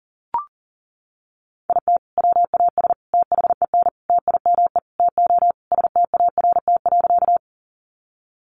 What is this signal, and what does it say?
Morse code, heard as a keyed tone, about 40 wpm, 700 Hz. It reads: IT WAS THEN TIME TO START.
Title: Sentences from Top 300 Words 40wpm